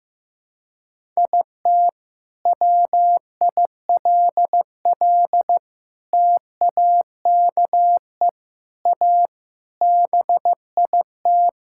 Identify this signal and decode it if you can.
Morse code, heard as a keyed tone, about 16 wpm, 700 Hz. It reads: IT WILL TAKE A BIT
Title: Sentences from Top 1000 Words - Review Part 5 - 15wpm